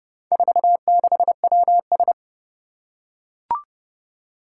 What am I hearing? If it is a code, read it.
Morse code, heard as a keyed tone, about 30 wpm, 700 Hz. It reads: 46WS
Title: Four Challenging Letter-Number Combinations 30wpm